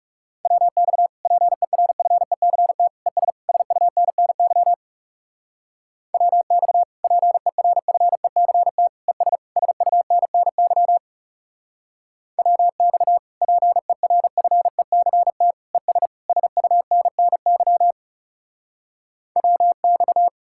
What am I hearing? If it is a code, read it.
Morse code, heard as a keyed tone, about 38 wpm, 700 Hz. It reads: WX PERFECT ES SUNNY WX PERFECT ES SUNNY WX PERFECT ES SUNNY WX